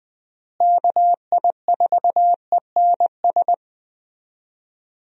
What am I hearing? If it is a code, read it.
Morse code, heard as a keyed tone, about 20 wpm, 700 Hz. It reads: KI4ENS
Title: USA Call Signs - 6 Characters - Repeated 3x - 20wpm